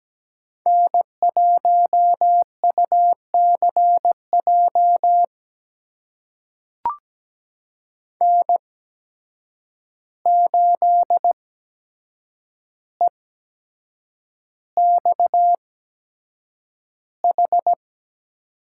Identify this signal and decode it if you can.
Morse code, heard as a keyed tone, about 17 wpm, spaced as for 7 wpm, 700 Hz. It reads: N1UCJ N8EXH
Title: USA Call Signs - 5 Characters - 8x Character Spacing - 17wpm